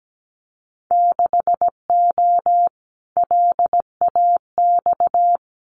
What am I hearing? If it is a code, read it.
Morse code, heard as a keyed tone, about 17 wpm, 700 Hz. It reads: 6O LAX
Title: Winter Field Day Exchanges Encore 17wpm